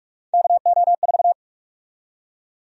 Morse code, heard as a keyed tone, about 46 wpm, 700 Hz. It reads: KO4
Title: USA Call Sign Prefixes - Repeated 2x - 45wpm